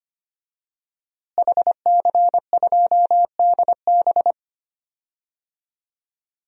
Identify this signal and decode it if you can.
Morse code, heard as a keyed tone, about 25 wpm, 700 Hz. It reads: HC2DB
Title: DXCC-100 Call Signs - 5 Characters - Repeated 2x - 25wpm